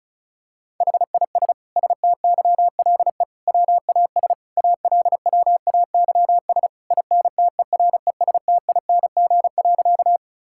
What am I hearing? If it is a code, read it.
Morse code, heard as a keyed tone, about 34 wpm, 700 Hz. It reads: HIS STYLE WAS ALWAYS INTERESTING.
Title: Sentences from Top 1000 Words - Review Part 5 - 35wpm